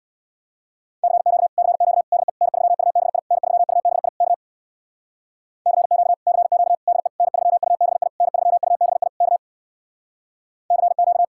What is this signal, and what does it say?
Morse code, heard as a keyed tone, about 64 wpm, 700 Hz. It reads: CQ CQ DE N3UBI N3UBI K CQ CQ DE N3UBI N3UBI K CQ